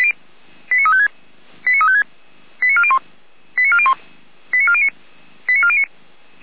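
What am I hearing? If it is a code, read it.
Five-tone selective call (ZVEI), heard as quick runs of five tones, tones of about 70 ms. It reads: 89356 89356 89401 89401 89409 89409
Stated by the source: ZVEI five-tone selective call received on SDR, Feuerwehr Orlamünde.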